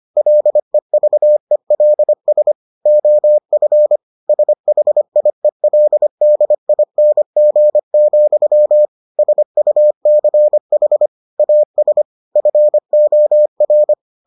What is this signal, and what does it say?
Morse code, heard as a keyed tone, about 25 wpm, 600 Hz. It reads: LEVELS OF SHIELDING, SUCH AS FOR